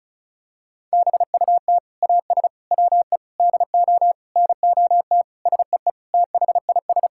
Morse code, heard as a keyed tone, about 36 wpm, 700 Hz. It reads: BUT AS WE DO NOT SEE THIS
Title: Sentences from Top 1600 Words 35wpm